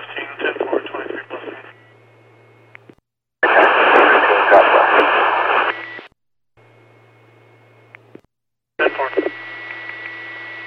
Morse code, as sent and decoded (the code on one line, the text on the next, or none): none